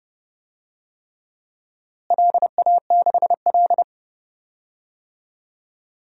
Morse code, heard as a keyed tone, about 30 wpm, 700 Hz. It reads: LA6L
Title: DXCC-100 Call Signs - 4 Characters - Repeated 3x - 30wpm